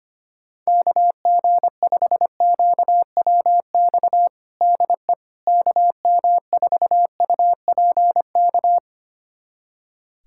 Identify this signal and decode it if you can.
Morse code, heard as a keyed tone, about 25 wpm, 700 Hz. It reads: KG5QWX DE KM4UPK
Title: USA Callsign DE Callsign - 6 Characters - 25wpm